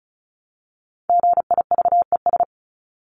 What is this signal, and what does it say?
Morse code, heard as a keyed tone, about 34 wpm, 700 Hz. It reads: GIVES
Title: ES:S Suffix - Top 100 Words with Base Words 35wpm